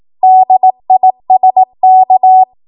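Morse code, heard as a keyed tone, about 18 wpm, 750 Hz. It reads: DISK
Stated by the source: Code practice at 18 WPM, a transition run bridging 15 to 20 WPM.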